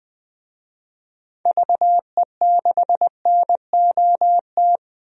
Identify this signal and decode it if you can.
Morse code, heard as a keyed tone, about 21 wpm, 700 Hz. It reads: VE6NOT
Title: DXCC-100 Call Signs - 6 Characters - 20wpm